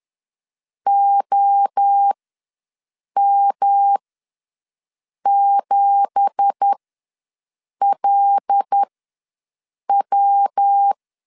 Morse code, heard as a keyed tone, about 10 wpm, spaced as for 6 wpm, 800 Hz. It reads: OM7LW
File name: LEZIONI / 9 2867 ws 10wpm 4215